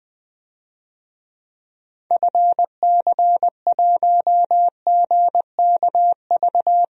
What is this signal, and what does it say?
Morse code, heard as a keyed tone, about 20 wpm, 700 Hz. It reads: FC1GKV
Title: DXCC-100 Call Signs - 6 Characters - Repeated 2x - 20wpm